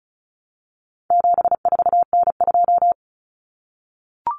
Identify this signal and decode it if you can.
Morse code, heard as a keyed tone, about 34 wpm, 700 Hz. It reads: 74N2
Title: Common Cut Numbers - 4-Digit 35wpm